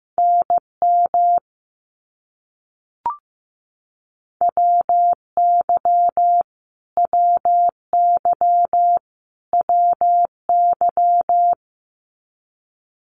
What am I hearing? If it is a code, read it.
Morse code, heard as a keyed tone, about 15 wpm, 700 Hz. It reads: NM WY WY WY
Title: US States Repeated 3x 15wpm